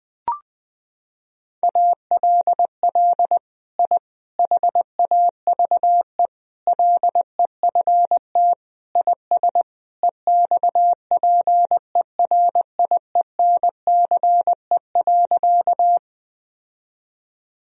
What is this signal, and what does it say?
Morse code, heard as a keyed tone, about 20 wpm, 700 Hz. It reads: ALL I HAVE LEFT IS EXPERIENCE.
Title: Sentences from Top 1000 Words - Review Part 3 - 20wpm